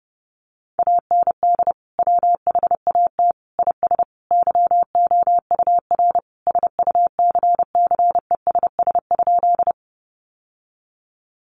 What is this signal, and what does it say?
Morse code, heard as a keyed tone, about 30 wpm, 700 Hz. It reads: AND WHAT IS YOUR SUCCESS?
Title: Sentences from Top 900 Words 30wpm